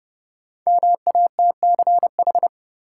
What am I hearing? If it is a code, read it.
Morse code, heard as a keyed tone, about 30 wpm, 700 Hz. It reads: MATCH
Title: Sentences from Top 1600 Words - New Word Drill - 30wpm